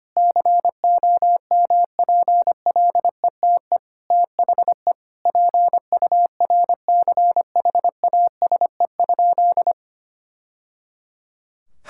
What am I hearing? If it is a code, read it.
Morse code, heard as a keyed tone, about 25 wpm, 700 Hz. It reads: COMPLETE THE PURCHASE?
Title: Sentences from Top 1000 Words - Review Part 5 - 25wpm